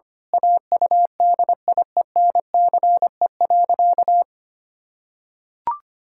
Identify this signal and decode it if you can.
Morse code, heard as a keyed tone, about 25 wpm, 700 Hz. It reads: AUDIENCE.
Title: Pangram Challenge v2 - 25wpm